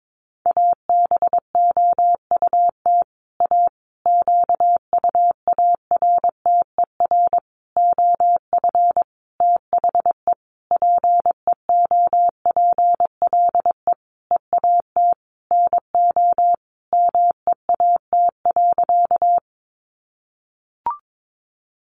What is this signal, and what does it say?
Morse code, heard as a keyed tone, about 22 wpm, 700 Hz. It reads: ABOUT A QUARTER OF THE PEOPLE EAT NO MEAT.
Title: Sentences from Top 1300 Words 22wpm